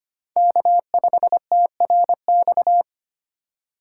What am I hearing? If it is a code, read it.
Morse code, heard as a keyed tone, about 25 wpm, 700 Hz. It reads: K5TRX
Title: USA Call Signs - 3x Character Spacing - 25wpm